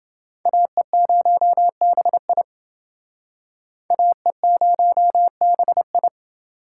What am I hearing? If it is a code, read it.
Morse code, heard as a keyed tone, about 28 wpm, 700 Hz. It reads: AE0BI AE0BI